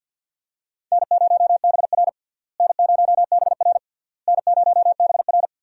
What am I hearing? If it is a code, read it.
Morse code, heard as a keyed tone, about 51 wpm, 700 Hz. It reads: N0BR N0BR N0BR
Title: USA Call Signs - 4 Characters - Repeated 3x - 50wpm